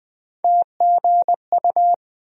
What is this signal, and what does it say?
Morse code, heard as a keyed tone, about 20 wpm, 700 Hz. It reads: TGU
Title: USA Call Sign Postfixes - 3 Letters 20wpm